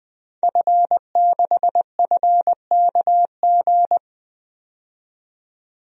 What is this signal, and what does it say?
Morse code, heard as a keyed tone, about 20 wpm, 700 Hz. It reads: F6FKG
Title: DXCC-100 Call Signs - 5 Characters - 20wpm